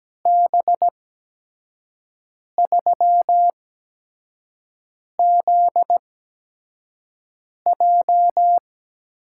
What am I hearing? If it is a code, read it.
Morse code, heard as a keyed tone, about 17 wpm, spaced as for 5 wpm, 700 Hz. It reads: B3ZJ
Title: USA Call Signs - 5 Characters - 8x Character Spacing - 17wpm